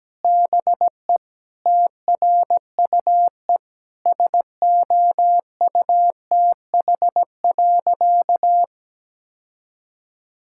Morse code, heard as a keyed tone, about 17 wpm, 700 Hz. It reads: BE TRUE SOUTH.
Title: Sentences from Top 1000 Words - Review Part 6 - 17wpm